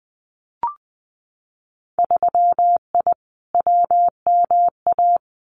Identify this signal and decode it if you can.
Morse code, heard as a keyed tone, about 20 wpm, 700 Hz. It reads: 3I WMA
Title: Winter Field Day Exchanges Encore 20wpm